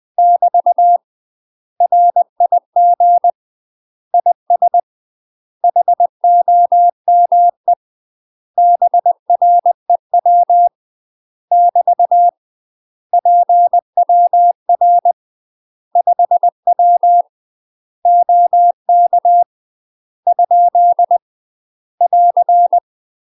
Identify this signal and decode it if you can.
Morse code, heard as a keyed tone, about 20 wpm, 700 Hz. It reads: <BT> RIG IS HOME BREW <BT> PWR 5W OK ? <AR>
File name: UK MASTER QSOs TRAINING / 20WPM / UK MASTER QSOs 0004